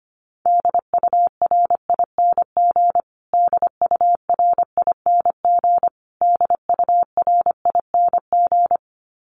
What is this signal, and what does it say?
Morse code, heard as a keyed tone, about 25 wpm, 700 Hz. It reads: DURING DURING DURING
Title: Top 500 Words Repeated 25wpm